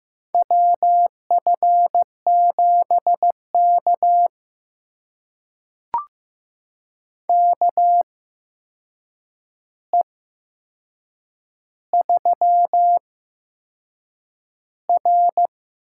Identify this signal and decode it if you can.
Morse code, heard as a keyed tone, about 15 wpm, spaced as for 6 wpm, 700 Hz. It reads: WF7K KE3R
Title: USA Call Signs - 4 Characters - 8x Character Spacing - 15wpm